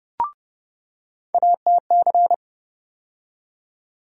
Morse code, heard as a keyed tone, about 30 wpm, 700 Hz. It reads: ATC
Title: USA Call Sign Postfixes - 3 Letters 30wpm